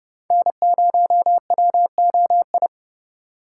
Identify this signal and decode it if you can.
Morse code, heard as a keyed tone, about 30 wpm, 700 Hz. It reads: N0WOI
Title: K1USN SST QSO Practice - 30wpm